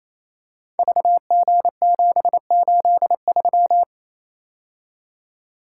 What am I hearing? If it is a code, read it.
Morse code, heard as a keyed tone, about 29 wpm, 700 Hz. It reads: VG783
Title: Code Groups - 5 Letter:Numbers - 28wpm